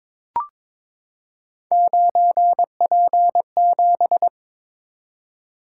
Morse code, heard as a keyed tone, about 22 wpm, 700 Hz. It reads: 9P7
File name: practice / 22wpm / Three Letter-Number Combinations - 22wpm